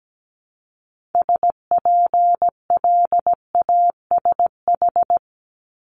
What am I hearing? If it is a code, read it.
Morse code, heard as a keyed tone, about 17 wpm, 700 Hz. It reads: SPLASH
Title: Multi-letter Phonemes Repeated 3x 17wpm